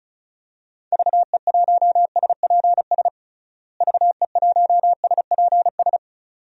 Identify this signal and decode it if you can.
Morse code, heard as a keyed tone, about 34 wpm, 700 Hz. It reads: VE1SPS VE1SPS
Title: DXCC-100 Call Signs - 6 Characters - Repeated 2x - 35wpm